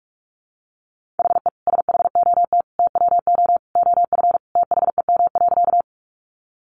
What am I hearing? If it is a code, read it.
Morse code, heard as a keyed tone, about 44 wpm, 700 Hz. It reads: HE SHOT TWO OF THEM.
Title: Sentences from Top 1100 Words 45wpm